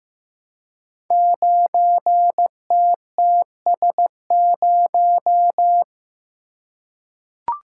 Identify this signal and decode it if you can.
Morse code, heard as a keyed tone, about 15 wpm, 700 Hz. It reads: 9TTS0
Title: Code Groups - 5 Letter:Numbers - 15wpm